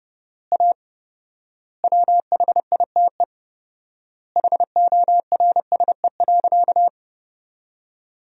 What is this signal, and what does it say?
Morse code, heard as a keyed tone, about 30 wpm, 700 Hz. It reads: A WHITE HORSE.
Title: 4x Word-Spacing - Sentences from Top 300 Words - 30wpm